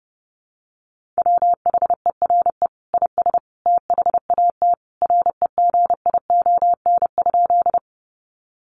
Morse code, heard as a keyed tone, about 30 wpm, 700 Hz. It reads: WHERE IS THAT REGION?
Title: Sentences from Top 1500 Words 30wpm